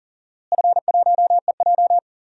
Morse code, heard as a keyed tone, about 39 wpm, 700 Hz. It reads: F1EJ
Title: DXCC-100 Call Signs - 4 Characters - Repeated 2x - 40wpm